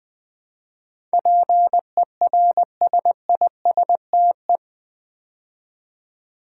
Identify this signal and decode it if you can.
Morse code, heard as a keyed tone, about 20 wpm, 700 Hz. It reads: PERSISTE
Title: Build A Word - 4 Letters At A Time - 10-Letter Words - 20wpm